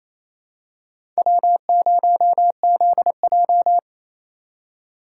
Morse code, heard as a keyed tone, about 27 wpm, 700 Hz. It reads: W0ZJ
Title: USA Call Signs - 4 Characters - 4x Character Spacing - 28wpm